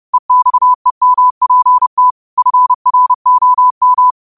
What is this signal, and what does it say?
Morse code, heard as a keyed tone, about 30 wpm, 1000 Hz. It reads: EXEMPT FROM
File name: qotdfiles / 30wpm